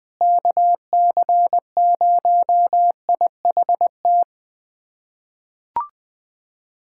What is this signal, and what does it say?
Morse code, heard as a keyed tone, about 20 wpm, 700 Hz. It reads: KC0IHT
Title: USA Callsign DE Callsign - 6 Characters - 20wpm